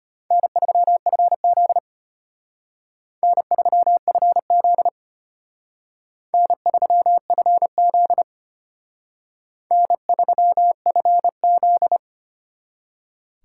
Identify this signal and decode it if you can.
Morse code, heard as a keyed tone, about 32 wpm, 700 Hz. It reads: N3FZ N3FZ N3FZ N3FZ